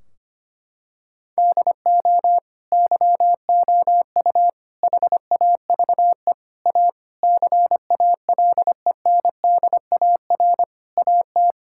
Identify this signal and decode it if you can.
Morse code, heard as a keyed tone, about 25 wpm, 700 Hz. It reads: DO YOU HAVE A CALENDAR AT
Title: Sentences from Top 1600 Words 25wpm